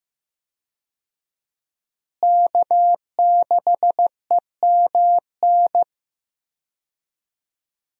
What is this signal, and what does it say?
Morse code, heard as a keyed tone, about 15 wpm, 700 Hz. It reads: K6EMN
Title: USA Call Signs - 5 Characters - Repeated 2x - 15wpm